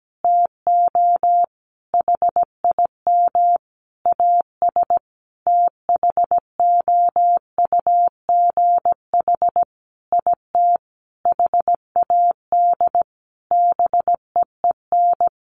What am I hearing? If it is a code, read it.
Morse code, heard as a keyed tone, about 17 wpm, 700 Hz. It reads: TO HIM AS THOUGH IT HAD BEEN
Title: Sentences from Top 1500 Words 17wpm